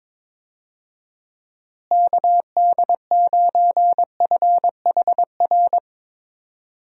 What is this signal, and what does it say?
Morse code, heard as a keyed tone, about 22 wpm, 700 Hz. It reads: KD9FHR
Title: USA Call Signs - 6 Characters - Repeated 2x - 22wpm